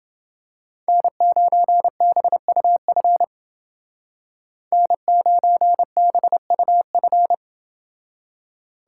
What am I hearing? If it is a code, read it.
Morse code, heard as a keyed tone, about 29 wpm, 700 Hz. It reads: N9BUF N9BUF